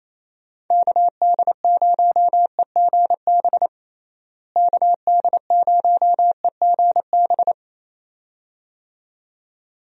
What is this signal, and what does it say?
Morse code, heard as a keyed tone, about 28 wpm, 700 Hz. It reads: KD0EGB KD0EGB
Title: USA Call Signs - 6 Characters - Repeated 2x - 28wpm